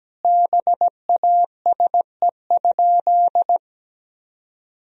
Morse code, heard as a keyed tone, about 17 wpm, 700 Hz. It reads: BASE?
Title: Sentences from Top 1000 Words - Review Part 1 - 17wpm